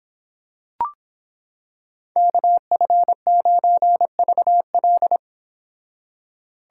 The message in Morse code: KF9VL

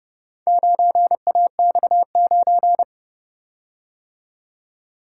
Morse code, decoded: 9AX9

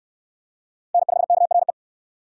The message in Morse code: N5KRE

